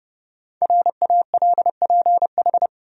Morse code, heard as a keyed tone, about 32 wpm, 700 Hz. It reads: RALPH
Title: USA Common Names - 30wpm